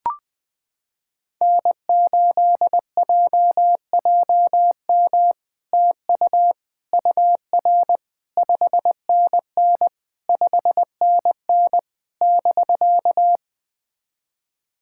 Morse code, heard as a keyed tone, about 20 wpm, 700 Hz. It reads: N8JJM TU UR 5NN 5NN <BK>
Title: POTA QSO Practice - 20wpm